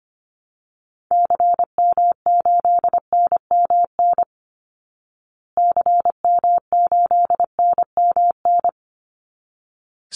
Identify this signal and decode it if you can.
Morse code, heard as a keyed tone, about 25 wpm, 700 Hz. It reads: CM8NMN CM8NMN
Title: Call Signs Repeated 3x - 25wpm